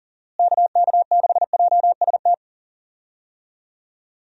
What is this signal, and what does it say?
Morse code, heard as a keyed tone, about 38 wpm, 700 Hz. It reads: KK6JST
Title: USA Call Signs - 6 Characters - Repeated 2x - 40wpm